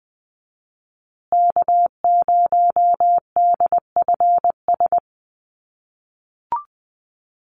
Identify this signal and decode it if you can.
Morse code, heard as a keyed tone, about 20 wpm, 700 Hz. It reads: K0DFS
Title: USA Call Signs Repeated 3x 20wpm